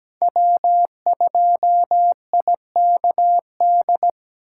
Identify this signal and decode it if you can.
Morse code, heard as a keyed tone, about 17 wpm, 700 Hz. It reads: W2IKD
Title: USA Call Signs - 6x Character Spacing - 17wpm